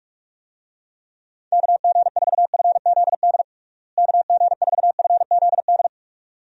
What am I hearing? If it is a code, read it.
Morse code, heard as a keyed tone, about 47 wpm, 700 Hz. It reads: KG4FZD KG4FZD
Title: USA Call Signs - 6 Characters - Repeated 3x - 45wpm